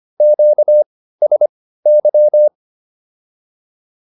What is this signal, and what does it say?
Morse code, heard as a keyed tone, about 25 wpm, spaced as for 15 wpm, 600 Hz. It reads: QSY